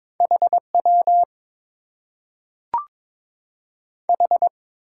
Morse code, heard as a keyed tone, about 22 wpm, 700 Hz. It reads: H W H
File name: Top 90 QSO Elements - 6x Character Spacing - 22wpm